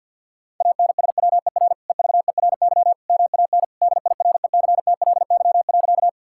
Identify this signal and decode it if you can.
Morse code, heard as a keyed tone, about 50 wpm, 690 Hz. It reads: ANSWER EVERY MAN DIRECTLY.
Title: Sentences from Top 1000 Words - Review Part 5 - 50wpm